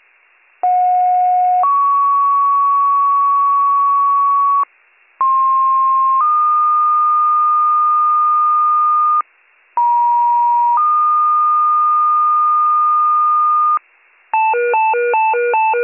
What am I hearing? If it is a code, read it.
Two-tone page, heard as a one-second tone, then a three-second tone, about 730 then 1120 Hz, about 1060 then 1220 Hz, about 950 then 1220 Hz. It is heard from 0.6 to 13.8 s.